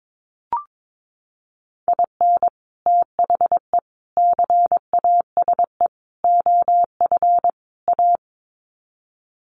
Morse code, heard as a keyed tone, about 22 wpm, 700 Hz. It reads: IN THE CASE OF A
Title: Sets of 5 Words Encore 22wpm